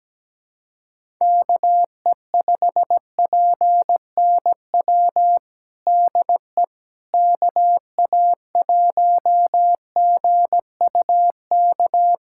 Morse code, heard as a keyed tone, about 17 wpm, 700 Hz. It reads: KE5PNW DE KA1GUK
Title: USA Callsign DE Callsign - 6 Characters - 17wpm